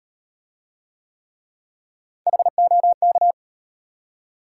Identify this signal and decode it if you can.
Morse code, heard as a keyed tone, about 36 wpm, 700 Hz. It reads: HOK